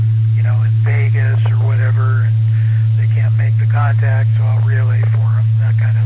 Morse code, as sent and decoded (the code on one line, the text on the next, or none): none